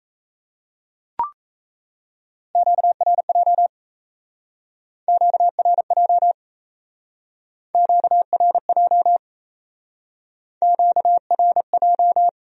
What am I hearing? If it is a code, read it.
Morse code, heard as a keyed tone, about 38 wpm, 700 Hz. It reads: QRJ QRJ QRJ QRJ